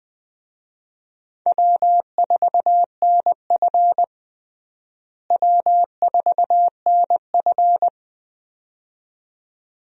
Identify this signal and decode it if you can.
Morse code, heard as a keyed tone, about 20 wpm, 700 Hz. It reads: W4NF W4NF
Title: USA Call Signs - 4 Characters - Repeated 2x - 20wpm